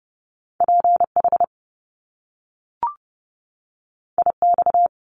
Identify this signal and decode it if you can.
Morse code, heard as a keyed tone, about 30 wpm, 700 Hz. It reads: PH IX
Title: Top Two Letter Combinations in Words 30wpm